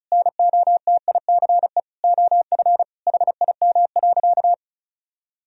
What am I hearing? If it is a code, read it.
Morse code, heard as a keyed tone, about 34 wpm, 700 Hz. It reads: NOTICE OF HIM.
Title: Sentences from Top 1000 Words - Review Part 4 - 35wpm